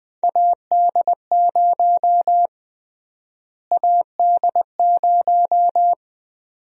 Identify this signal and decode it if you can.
Morse code, heard as a keyed tone, about 20 wpm, 700 Hz. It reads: AD0 AD0
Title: USA Call Sign Prefixes - Repeated 2x - 20wpm